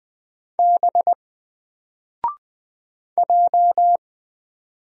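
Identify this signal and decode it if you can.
Morse code, heard as a keyed tone, about 20 wpm, 700 Hz. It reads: B J
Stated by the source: Single letter and number Morse drill at 20 wpm, from N4LCD.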